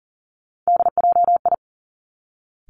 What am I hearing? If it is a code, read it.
Morse code, heard as a keyed tone, about 41 wpm, 700 Hz. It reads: DJI
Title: Three Challenging Letter-Number Combinations 40wpm